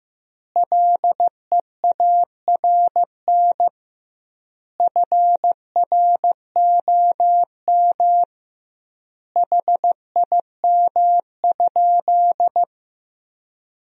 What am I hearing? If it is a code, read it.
Morse code, heard as a keyed tone, about 16 wpm, 700 Hz. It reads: LEARN FROM HIM?
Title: 2x Word-Spacing - Sentences from Top 300 Words - 15wpm